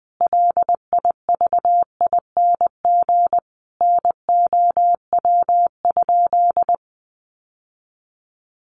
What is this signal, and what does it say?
Morse code, heard as a keyed tone, about 19 wpm, 700 Hz. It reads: LIVING NOW?
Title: Sentences from Top 1100 Words 20wpm